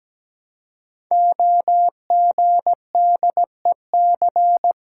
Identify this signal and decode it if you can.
Morse code, heard as a keyed tone, about 17 wpm, 700 Hz. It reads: OGDEC
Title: Code Groups - 5 Letters - 17wpm